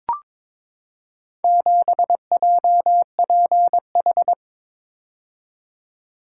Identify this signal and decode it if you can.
Morse code, heard as a keyed tone, about 23 wpm, 690 Hz. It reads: 7JPH
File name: practice / 22wpm / Four Letter-Number Combinations - 22wpm